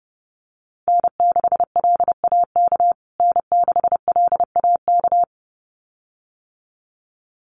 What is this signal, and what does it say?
Morse code, heard as a keyed tone, about 30 wpm, 700 Hz. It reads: N6LAK N6LAK
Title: USA Call Signs Repeated 2x 30wpm